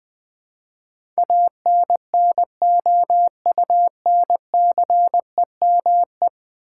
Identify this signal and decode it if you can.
Morse code, heard as a keyed tone, about 21 wpm, 700 Hz. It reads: ANNOUNCEME
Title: Build A Word - 5 Letters At A Time - 12-Letter Words - 20wpm